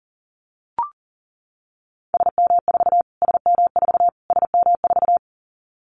Morse code, heard as a keyed tone, about 40 wpm, 700 Hz. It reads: SM4 SM4 SM4
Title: Callsign Prefixes for Top 25 DXCC - Repeated 3x - 40wpm